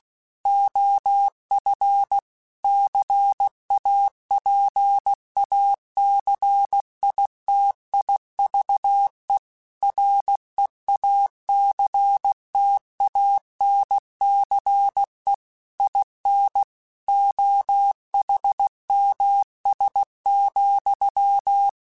Morse code, heard as a keyed tone, about 16 wpm, 800 Hz. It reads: OF CAPACITIVE REACTANCE IN OHMS,